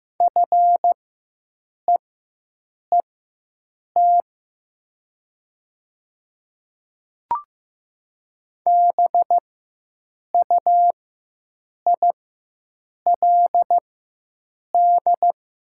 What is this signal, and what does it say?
Morse code, heard as a keyed tone, about 15 wpm, spaced as for 7 wpm, 700 Hz. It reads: FEET BUILD
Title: 4x Character Spacing - Top 500 Words - 15wpm